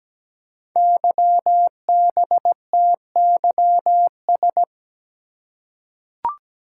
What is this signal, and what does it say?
Morse code, heard as a keyed tone, about 17 wpm, 700 Hz. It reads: YBTYS
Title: Code Groups - 5 Letters - 17wpm